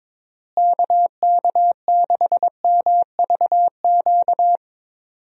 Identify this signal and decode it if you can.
Morse code, heard as a keyed tone, about 22 wpm, 700 Hz. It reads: KK6MVQ
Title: USA Call Signs - 6 Characters - Repeated 3x - 22wpm